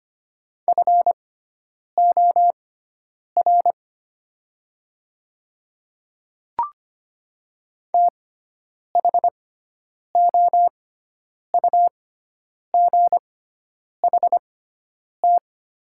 Morse code, heard as a keyed tone, about 25 wpm, spaced as for 9 wpm, 700 Hz. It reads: FOR THOUGHT
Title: 6x Character Spacing - Top 500 Words - 25wpm